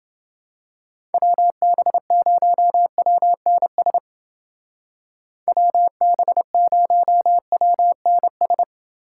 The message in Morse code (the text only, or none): WB0WNS WB0WNS